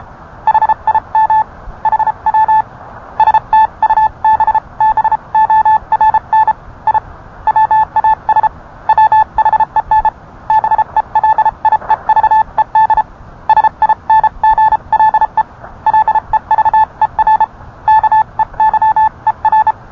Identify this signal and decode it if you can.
Morse code, heard as a keyed tone, about 31 wpm, 870 Hz. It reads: HIM HW STUBBORN I WAS WHEN BELIEVED SINGLE LEVER KEYER